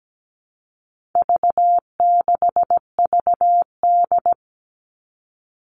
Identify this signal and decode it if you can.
Morse code, heard as a keyed tone, about 17 wpm, 700 Hz. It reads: V6VD
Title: All Cut Numbers - 4-Digit 17wpm